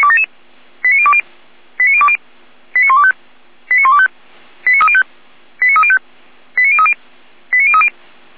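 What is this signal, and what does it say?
Five-tone selective call (ZVEI), heard as quick runs of five tones, tones of about 70 ms. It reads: 89020 89020 89125 89125 89305 89305 89030 89030
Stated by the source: ZVEI five-tone selective call received on an SDR.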